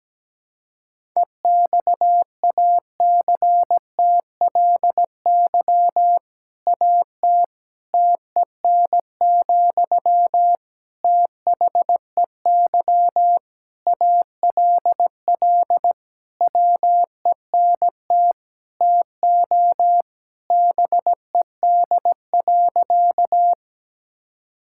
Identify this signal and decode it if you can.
Morse code, heard as a keyed tone, about 17 wpm, 700 Hz. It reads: EXACTLY AT TEN, THEY ALL WENT TO BED.
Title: Sentences from Top 1000 Words - Review Part 6 - 17wpm